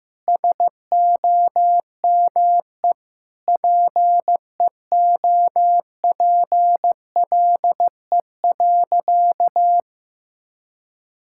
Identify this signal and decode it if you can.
Morse code, heard as a keyed tone, about 15 wpm, 700 Hz. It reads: SOME PEOPLE.
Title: Sentences from Top 1000 Words - Review Part 5 - 15wpm